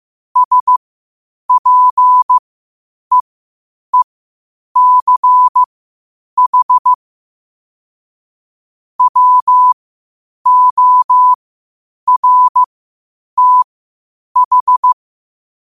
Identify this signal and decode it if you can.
Morse code, heard as a keyed tone, about 15 wpm, spaced as for 8 wpm, 1000 Hz. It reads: SPEECH WORTH
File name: qotdfiles / 8wpm